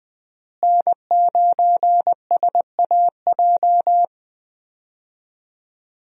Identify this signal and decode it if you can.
Morse code, heard as a keyed tone, about 21 wpm, 700 Hz. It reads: N9SAJ
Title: USA Call Signs - 5 Characters - Repeated 3x - 20wpm